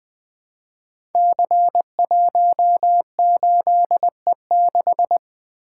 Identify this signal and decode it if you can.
Morse code, heard as a keyed tone, about 20 wpm, 700 Hz. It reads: C18E6
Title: Code Groups - 5 Letter:Numbers - 20wpm